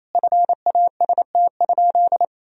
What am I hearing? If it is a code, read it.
Morse code, heard as a keyed tone, about 28 wpm, 700 Hz. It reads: FAST?
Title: Sentences from Top 1000 Words - Review Part 2 - 28wpm